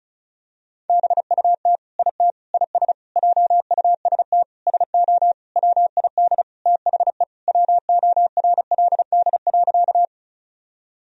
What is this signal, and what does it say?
Morse code, heard as a keyed tone, about 35 wpm, 700 Hz. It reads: BUT IT IS JUST SO WID THE WORLD.
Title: Sentences from Top 200 Words and Abbreviations 35wpm